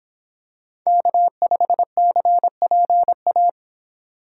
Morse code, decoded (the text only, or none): K5CPA